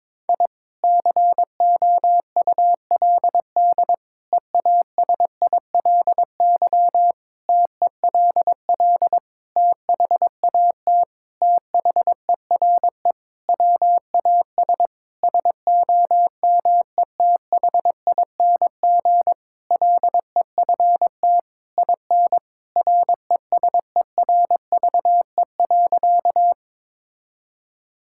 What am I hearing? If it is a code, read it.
Morse code, heard as a keyed tone, about 22 wpm, 700 Hz. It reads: I COULD EASILY TELL THAT THERE WAS SOMETHING LEFT IN RESERVE.
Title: Sentences from Top 1000 Words - Review Part 6 - 22wpm